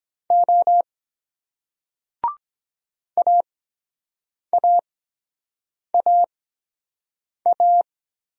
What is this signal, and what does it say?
Morse code, heard as a keyed tone, about 23 wpm, spaced as for 7 wpm, 700 Hz. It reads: O AAAA